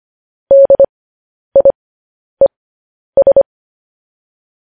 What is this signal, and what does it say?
Morse code, heard as a keyed tone, about 25 wpm, spaced as for 10 wpm, 550 Hz. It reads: DIES